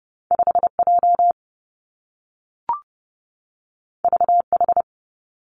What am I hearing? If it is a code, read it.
Morse code, heard as a keyed tone, about 30 wpm, 700 Hz. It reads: 5 J V H
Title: Two Challenging Letter-Number Combinations 30wpm